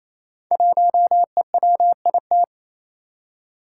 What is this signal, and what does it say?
Morse code, heard as a keyed tone, about 28 wpm, 700 Hz. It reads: 1EWIT
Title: Five Challenging Letter-Number Combinations 28wpm